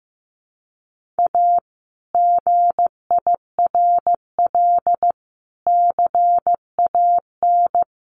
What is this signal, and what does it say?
Morse code, heard as a keyed tone, about 15 wpm, 700 Hz. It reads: A GIRL CAN
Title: Sentences from Top 1400 Words 15wpm